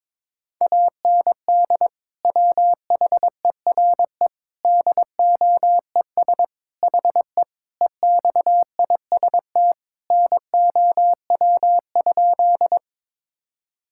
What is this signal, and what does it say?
Morse code, heard as a keyed tone, about 22 wpm, 700 Hz. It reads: AND WHERE DOES HE EXIST NOW?
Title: Sentences from Top 1000 Words - Review Part 5 - 22wpm